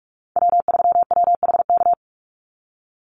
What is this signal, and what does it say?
Morse code, heard as a keyed tone, about 44 wpm, 700 Hz. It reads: W3WHK
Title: Call Signs - 45wpm